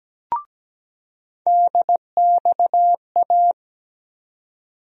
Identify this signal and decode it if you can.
Morse code, heard as a keyed tone, about 17 wpm, 700 Hz. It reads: DXA
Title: USA Call Sign Postfixes - 3 Letters 17wpm